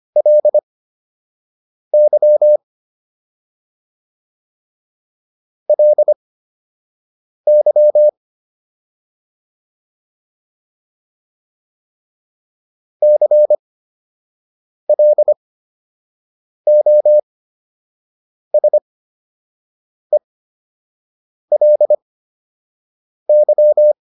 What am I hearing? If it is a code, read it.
Morse code, heard as a keyed tone, about 25 wpm, spaced as for 6 wpm, 600 Hz. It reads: LY LY CLOSELY